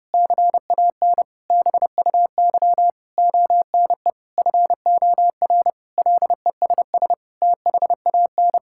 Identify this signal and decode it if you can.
Morse code, heard as a keyed tone, about 30 wpm, 700 Hz. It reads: CAN BUY ONE FOR LESS THAN
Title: Sentences from Top 1000 Words - Review Part 5 - 30wpm